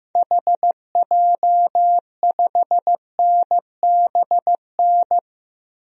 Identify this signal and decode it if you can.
Morse code, heard as a keyed tone, about 15 wpm, 700 Hz. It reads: HJ5NBN
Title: DXCC-100 Call Signs - 6 Characters - Repeated 2x - 15wpm